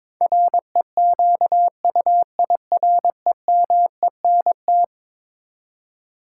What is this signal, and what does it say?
Morse code, heard as a keyed tone, about 22 wpm, 700 Hz. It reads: REQUIREMENT
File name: practice / 22wpm / Eleven Letter Words - 22wpm